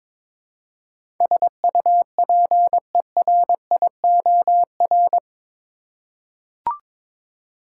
Morse code, heard as a keyed tone, about 22 wpm, 700 Hz. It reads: SUPERIOR
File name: practice / 22wpm / Eight Letter Words - 22wpm